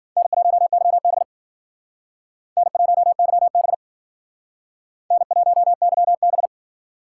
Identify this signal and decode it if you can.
Morse code, heard as a keyed tone, about 51 wpm, 700 Hz. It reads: N1YB N1YB N1YB